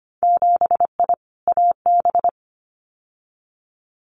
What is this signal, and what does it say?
Morse code, heard as a keyed tone, about 25 wpm, 700 Hz. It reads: 7I AB
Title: Winter Field Day Exchanges Encore 25wpm